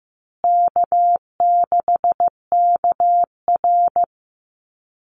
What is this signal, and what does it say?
Morse code, heard as a keyed tone, about 15 wpm, 700 Hz. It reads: K6KR
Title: Easy Call Signs Repeated 3x - 15wpm